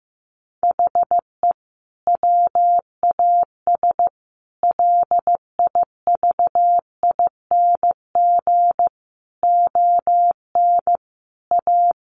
Morse code, heard as a keyed tone, about 15 wpm, 700 Hz. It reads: HE WAS LIVING ON A
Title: Sentences from Top 1200 Words 15wpm